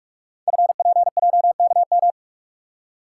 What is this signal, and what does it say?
Morse code, heard as a keyed tone, about 44 wpm, 700 Hz. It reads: FPJKM
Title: Code Groups - 5 Letter:Numbers - 45wpm